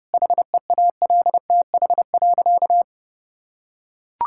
Morse code, heard as a keyed tone, about 30 wpm, 700 Hz. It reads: HEALTH.
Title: Sentences from Top 1000 Words - Review Part 6 - 30wpm